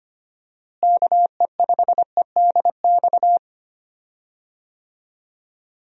Morse code, heard as a keyed tone, about 25 wpm, 700 Hz. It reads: KE5EDX
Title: USA Call Signs - 6 Characters - Repeated 2x - 25wpm